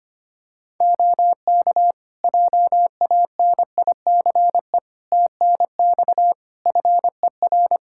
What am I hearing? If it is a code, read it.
Morse code, heard as a keyed tone, about 25 wpm, 700 Hz. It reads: OK JANICE TNX FER
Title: K1USN SST QSO Practice 25wpm